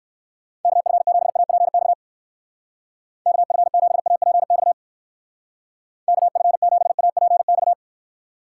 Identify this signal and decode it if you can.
Morse code, heard as a keyed tone, about 55 wpm, 700 Hz. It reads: KF7APX KF7APX KF7APX
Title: Speed Racing - USA 2x3 Call Signs 45wpm